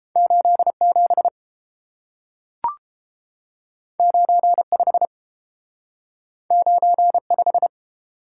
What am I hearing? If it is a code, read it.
Morse code, heard as a keyed tone, about 30 wpm, 700 Hz. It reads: 87 95 95